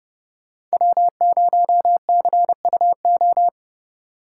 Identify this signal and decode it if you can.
Morse code, heard as a keyed tone, about 30 wpm, 700 Hz. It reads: W0CUO
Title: Speed Racing - USA 1x3 Call Signs 25wpm